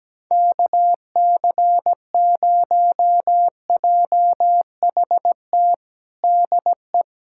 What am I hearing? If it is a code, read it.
Morse code, heard as a keyed tone, about 17 wpm, 700 Hz. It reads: KC0JHT DE
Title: USA Callsign DE Callsign - 6 Characters - 17wpm